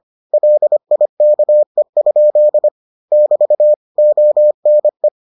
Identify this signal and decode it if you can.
Morse code, heard as a keyed tone, about 25 wpm, 600 Hz. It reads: LIKE? <BT> ONE